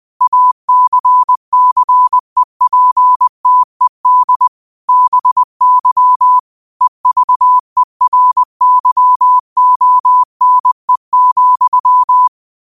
Morse code, heard as a keyed tone, about 20 wpm, 1000 Hz. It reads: ACCEPTED BY EVERYONE,